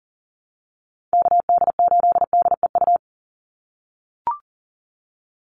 Morse code, heard as a keyed tone, about 40 wpm, 700 Hz. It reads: KD8DEU
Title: NA SOTA Chaser Callsigns 40wpm